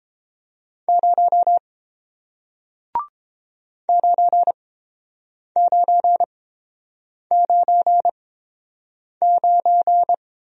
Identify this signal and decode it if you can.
Morse code, heard as a keyed tone, about 30 wpm, spaced as for 7 wpm, 700 Hz. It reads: 0 9999